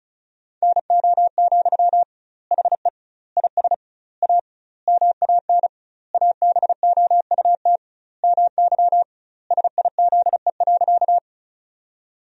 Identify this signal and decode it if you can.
Morse code, heard as a keyed tone, about 34 wpm, 700 Hz. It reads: NO, HE IS A MAN ABOUT MY SIZE.
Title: 2x Word-Spacing - Sentences from Top 400 Words - 35wpm